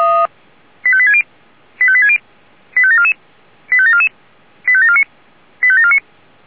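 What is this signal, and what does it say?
Five-tone selective call (ZVEI), heard as quick runs of five tones, tones of about 70 ms. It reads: 85860 85860 85744 85744 85749 85749
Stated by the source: Five-tone ZVEI selective call received on an SDR.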